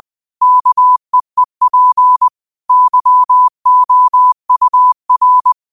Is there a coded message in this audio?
Yes: Morse code from 0.4 to 5.5 s, heard as a keyed tone, about 20 wpm, 1000 Hz. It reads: KEEP YOUR